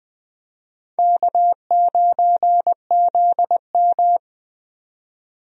Morse code, heard as a keyed tone, about 20 wpm, 700 Hz. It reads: K9ZM
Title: USA Call Signs - 4 Characters - 20wpm